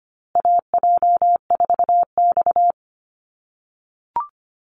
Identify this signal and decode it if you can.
Morse code, heard as a keyed tone, about 25 wpm, 700 Hz. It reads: AJ4X
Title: USA Call Signs Repeated 2x 25wpm